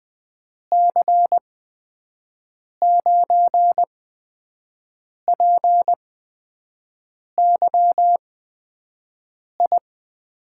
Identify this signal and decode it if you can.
Morse code, heard as a keyed tone, about 20 wpm, spaced as for 5 wpm, 700 Hz. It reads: C9PYI
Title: USA Call Signs - 6 Characters - 8x Character Spacing - 20wpm